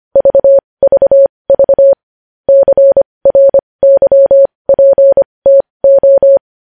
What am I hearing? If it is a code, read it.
Morse code, heard as a keyed tone, about 25 wpm, spaced as for 20 wpm, 550 Hz. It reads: VVV CRYPTO
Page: news-headlines-in-morse-code-at-20-wpm